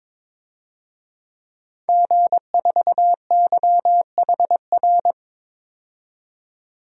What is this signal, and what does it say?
Morse code, heard as a keyed tone, about 22 wpm, 700 Hz. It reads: G4YHR
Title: DXCC-100 Call Signs - 5 Characters - 22wpm